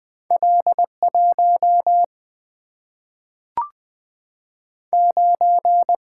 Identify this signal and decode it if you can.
Morse code, heard as a keyed tone, about 20 wpm, 700 Hz. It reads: L 1 9
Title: Single-Letter Single-Number 20wpm